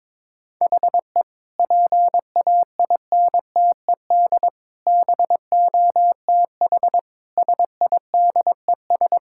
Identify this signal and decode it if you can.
Morse code, heard as a keyed tone, about 22 wpm, 700 Hz. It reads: HE PAINTED BOTH SIDES
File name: practice / 22wpm / Sentences with 4-Word Phrases - 22wpm